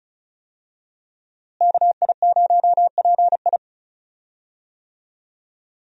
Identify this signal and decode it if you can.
Morse code, heard as a keyed tone, about 34 wpm, 700 Hz. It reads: KI0PI
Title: USA Call Signs - 5 Characters - Repeated 2x - 35wpm